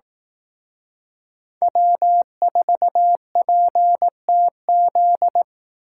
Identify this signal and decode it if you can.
Morse code, heard as a keyed tone, about 18 wpm, 700 Hz. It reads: W4PTZ
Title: Speed Racing - USA Call Signs 15wpm